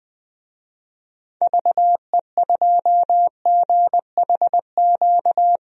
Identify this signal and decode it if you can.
Morse code, heard as a keyed tone, about 20 wpm, 700 Hz. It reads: VE2GHQ
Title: DXCC-100 Call Signs - 6 Characters - 20wpm